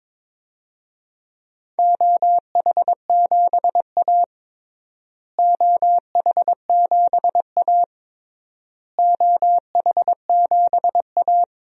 Morse code, heard as a keyed tone, about 22 wpm, 700 Hz. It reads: OH7A OH7A OH7A
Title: DXCC-100 Call Signs - 4 Characters - Repeated 3x - 22wpm